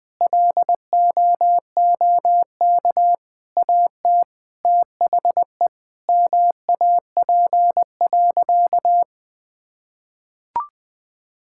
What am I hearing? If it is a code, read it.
Morse code, heard as a keyed tone, about 20 wpm, 700 Hz. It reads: LOOK AT THE MAP.
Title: Sentences with 5-Word Phrases - 20wpm